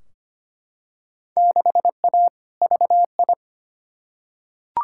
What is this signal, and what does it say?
Morse code, heard as a keyed tone, about 25 wpm, 700 Hz. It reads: 6A VI
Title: Field Day Exchanges Encore - 25wpm